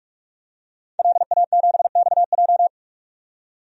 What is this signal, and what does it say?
Morse code, heard as a keyed tone, about 46 wpm, 700 Hz. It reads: LA7XJ
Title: DXCC-100 Call Signs - 5 Characters - 45wpm